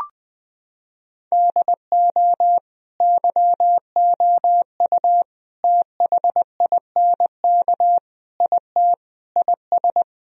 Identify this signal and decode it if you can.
Morse code, heard as a keyed tone, about 21 wpm, 700 Hz. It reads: DO YOU THINK IT IS
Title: Sentences from Top 1000 Words - Review Part 1 - 20wpm